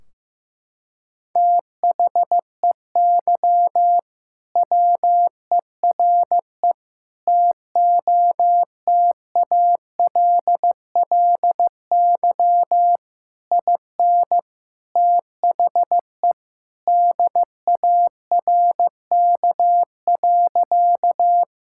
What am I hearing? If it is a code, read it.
Morse code, heard as a keyed tone, about 16 wpm, 700 Hz. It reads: THEY WERE TOTALLY IN THE DARK.
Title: Sentences from Top 1600 Words 15wpm